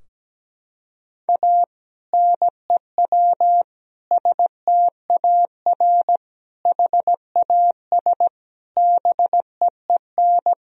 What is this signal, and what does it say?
Morse code, heard as a keyed tone, about 17 wpm, 700 Hz. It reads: A NEW STAR HAS BEEN
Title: Sentences from Top 1000 Words - Review Part 2 - 17wpm